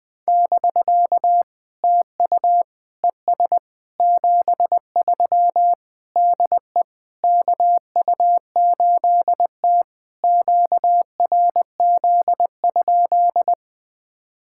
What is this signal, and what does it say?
Morse code, heard as a keyed tone, about 20 wpm, 690 Hz. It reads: <BK> TU ES 73 DE KU8T QRZ?
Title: POTA QSO Practice - 20wpm